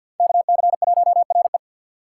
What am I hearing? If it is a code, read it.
Morse code, heard as a keyed tone, about 48 wpm, 690 Hz. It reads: KC1RE
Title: USA Call Signs - 6x Character Spacing - 50wpm